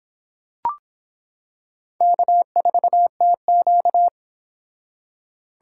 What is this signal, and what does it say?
Morse code, heard as a keyed tone, about 26 wpm, 700 Hz. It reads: K4TQ